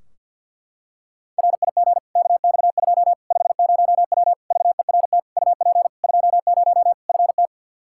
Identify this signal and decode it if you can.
Morse code, heard as a keyed tone, about 46 wpm, 700 Hz. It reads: RIG KX2 50W VERT UP 20 FT
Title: Practice QSO - Exchange 2 - Info - 50wpm